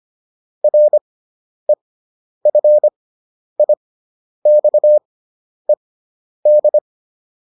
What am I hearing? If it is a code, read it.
Morse code, heard as a keyed tone, about 25 wpm, spaced as for 10 wpm, 600 Hz. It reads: REFIXED